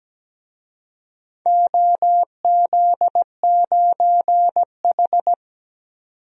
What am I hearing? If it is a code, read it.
Morse code, heard as a keyed tone, about 17 wpm, 700 Hz. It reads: OZ9H
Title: DXCC-100 Call Signs - 4 Characters - 17wpm